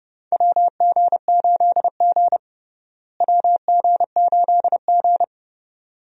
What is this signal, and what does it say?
Morse code, heard as a keyed tone, about 32 wpm, 700 Hz. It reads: WG8G WG8G
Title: USA Call Signs - 4 Characters - Repeated 2x - 30wpm